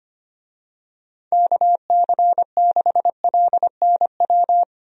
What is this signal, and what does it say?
Morse code, heard as a keyed tone, about 25 wpm, 700 Hz. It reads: KC6LNW
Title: USA Call Signs - 6 Characters - 25wpm